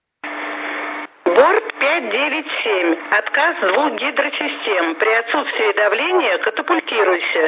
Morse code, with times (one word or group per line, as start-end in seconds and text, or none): none